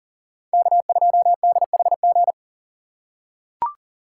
Morse code, heard as a keyed tone, about 40 wpm, 700 Hz. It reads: K2DHG
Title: USA Call Signs - 5 Characters - 6x Character Spacing - 40wpm